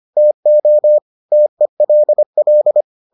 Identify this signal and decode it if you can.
Morse code, heard as a keyed tone, about 25 wpm, 600 Hz. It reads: TO TELL